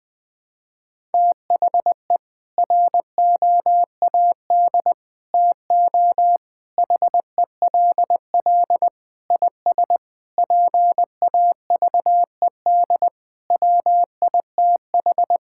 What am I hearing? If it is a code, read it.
Morse code, heard as a keyed tone, about 20 wpm, 700 Hz. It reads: THE ROAD TO HELL IS PAVED WITH
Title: Sentences with 8-Word Phrases - 20wpm